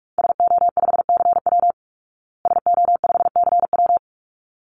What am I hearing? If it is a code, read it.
Morse code, heard as a keyed tone, about 44 wpm, 700 Hz. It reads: SO5CW SO5CW
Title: Call Signs Repeated 3x - 45wpm